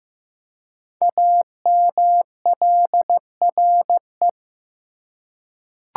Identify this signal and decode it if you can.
Morse code, heard as a keyed tone, about 15 wpm, 690 Hz. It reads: AMLRE
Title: Code Groups - 5 Letters - 15wpm